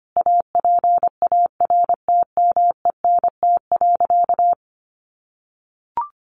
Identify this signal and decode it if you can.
Morse code, heard as a keyed tone, about 25 wpm, 700 Hz. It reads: APARTMENT.
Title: Sentences from Top 900 Words 25wpm